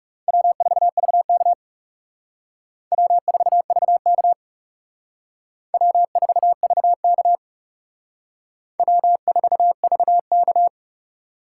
Morse code, heard as a keyed tone, about 36 wpm, 700 Hz. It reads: W4VK W4VK W4VK W4VK
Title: Speed Racing - USA 1x2 Call Signs 30wpm